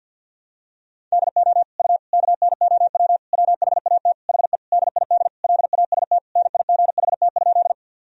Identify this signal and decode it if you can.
Morse code, heard as a keyed tone, about 52 wpm, 700 Hz. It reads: DO U KNOW WHAT HE DID LAST NIGHT?
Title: Sentences from Top 300 Words and Abbreviations 50wpm